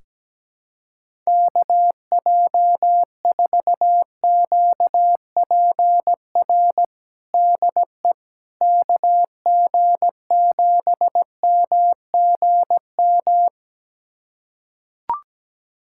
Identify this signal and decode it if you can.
Morse code, heard as a keyed tone, about 17 wpm, 700 Hz. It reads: KJ4QPR DE KG7MGM
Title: USA Callsign DE Callsign 17wpm